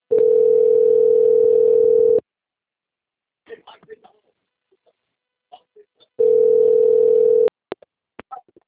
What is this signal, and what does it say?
Ringback tone (North America), 440 + 480 Hz, with speech.
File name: out- - - - - .34468